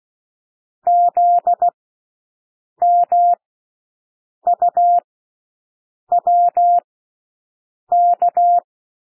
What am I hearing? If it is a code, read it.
Morse code, heard as a keyed tone, about 16 wpm, spaced as for 6 wpm, 700 Hz. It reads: ZMUWK
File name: cwmp3 / L1 SOUND LETTERS 16 wpm